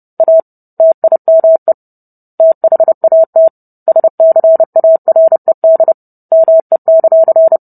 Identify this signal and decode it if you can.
Morse code, heard as a keyed tone, about 30 wpm, 650 Hz. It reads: A TIME THAT SCARED ME;